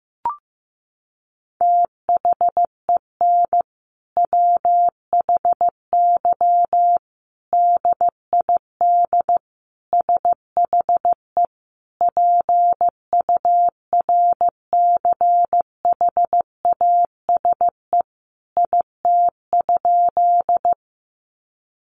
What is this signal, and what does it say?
Morse code, heard as a keyed tone, about 15 wpm, 700 Hz. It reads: THEN WHY DID SHE PURCHASE IT?
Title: Sentences from Top 800 Words 15wpm